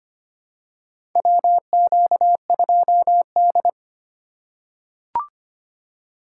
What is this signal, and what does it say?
Morse code, heard as a keyed tone, about 25 wpm, 700 Hz. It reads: WQ2D
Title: USA Call Signs - 2x Character Spacing - 25wpm